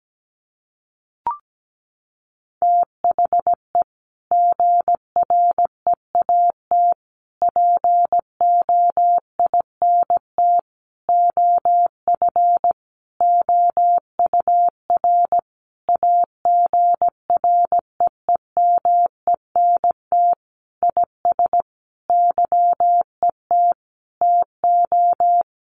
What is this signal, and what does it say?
Morse code, heard as a keyed tone, about 17 wpm, 700 Hz. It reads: THE GREAT POINT OF OUR AGREEMENT IS YET TO
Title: Sentences from Top 1500 Words 17wpm